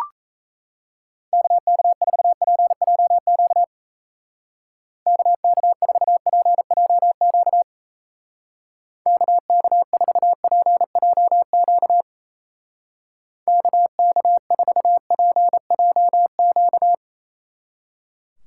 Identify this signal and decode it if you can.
Morse code, heard as a keyed tone, about 33 wpm, 700 Hz. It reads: KK4PJQ KK4PJQ KK4PJQ KK4PJQ
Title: Speed Racing - USA 2x3 Call Signs 28wpm